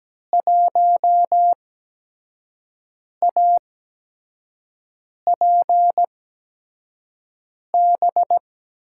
Morse code, heard as a keyed tone, about 17 wpm, spaced as for 5 wpm, 700 Hz. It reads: 1APB
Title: USA Call Signs - 6 Characters - 8x Character Spacing - 17wpm